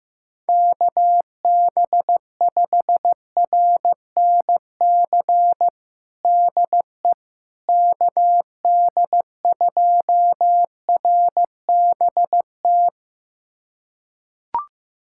Morse code, heard as a keyed tone, about 16 wpm, 700 Hz. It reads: KB5RNC DE KD2RBT
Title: USA Callsign DE Callsign - 6 Characters - 15wpm